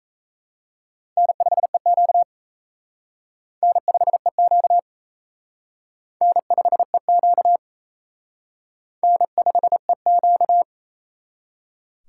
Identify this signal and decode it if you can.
Morse code, heard as a keyed tone, about 35 wpm, 700 Hz. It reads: N5EQ N5EQ N5EQ N5EQ
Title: Speed Racing - USA 1x2 Call Signs 28wpm